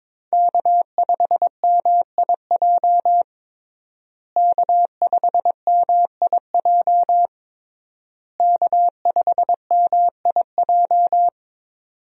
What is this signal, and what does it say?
Morse code, heard as a keyed tone, about 21 wpm, 700 Hz. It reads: K5MIJ K5MIJ K5MIJ
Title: USA Call Signs - 5 Characters - Repeated 3x - 22wpm